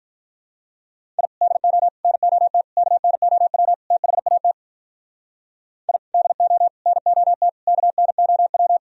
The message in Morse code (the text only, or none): I DO NOT KNOW THAT I DO NOT KNOW